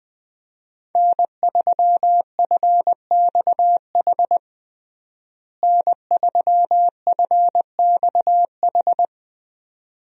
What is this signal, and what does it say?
Morse code, heard as a keyed tone, about 20 wpm, 700 Hz. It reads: N3FXH N3FXH
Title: USA Call Signs - 5 Characters - Repeated 2x - 20wpm